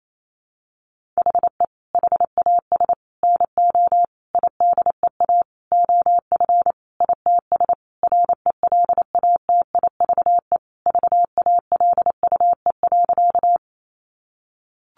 Morse code, heard as a keyed tone, about 28 wpm, 700 Hz. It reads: HE HAS NO IDEA OF ITS RELATIVE VALUE.
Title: Sentences from Top 1400 Words 28wpm